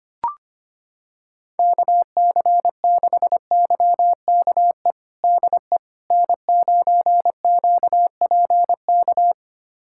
Morse code, heard as a keyed tone, about 25 wpm, 700 Hz. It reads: KC6YKE DE N9QPK
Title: USA Callsign DE Callsign 25wpm